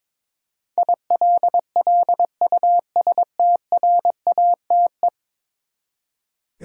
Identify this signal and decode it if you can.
Morse code, heard as a keyed tone, about 22 wpm, 700 Hz. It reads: ILLUSTRATE
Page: morse-code-ninja-practice-22wpm